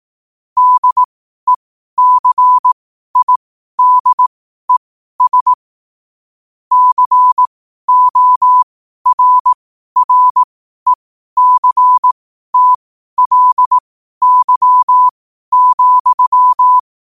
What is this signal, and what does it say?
Morse code, heard as a keyed tone, about 18 wpm, spaced as for 12 wpm, 1000 Hz. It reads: DECIDES CORRECTLY,
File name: qotdfiles / 12wpm